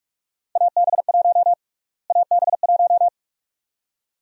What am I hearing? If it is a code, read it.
Morse code, heard as a keyed tone, about 44 wpm, 700 Hz. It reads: AB1 AB1
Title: USA Call Sign Prefixes - Repeated 2x - 45wpm